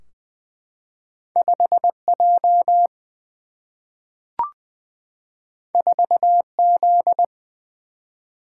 Morse code, heard as a keyed tone, about 20 wpm, 700 Hz. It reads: 5 J 4 Z
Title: Single-Letter Single-Number 20wpm